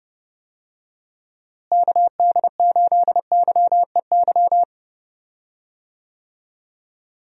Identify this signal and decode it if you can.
Morse code, heard as a keyed tone, about 30 wpm, 700 Hz. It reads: KD8YEY
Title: USA Call Signs - 6 Characters - Repeated 2x - 30wpm